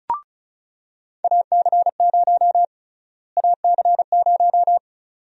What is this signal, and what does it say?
Morse code, heard as a keyed tone, about 36 wpm, 700 Hz. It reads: AC0 AC0
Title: USA Call Sign Prefixes - Repeated 2x - 35wpm